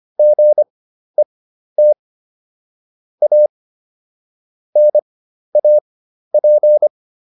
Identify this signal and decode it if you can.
Morse code, heard as a keyed tone, about 25 wpm, spaced as for 12 wpm, 600 Hz. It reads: GET A NAP